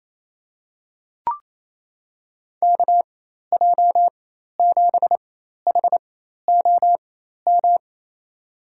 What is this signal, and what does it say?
Morse code, heard as a keyed tone, about 27 wpm, spaced as for 13 wpm, 700 Hz. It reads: KJ7HOM
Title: USA Call Signs - 4x Character Spacing - 28wpm